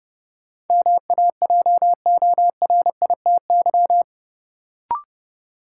Morse code, heard as a keyed tone, about 32 wpm, 700 Hz. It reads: MAJORITY
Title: Sets of 3 Words - New Word Drill - 30wpm